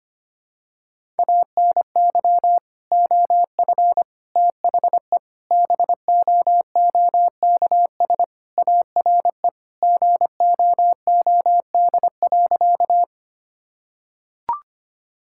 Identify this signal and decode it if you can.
Morse code, heard as a keyed tone, about 25 wpm, 700 Hz. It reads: ANY OF THE BOOKS ARE GOOD.
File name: practice / 25wpm / Sentences with 3-Word Phrases - 25wpm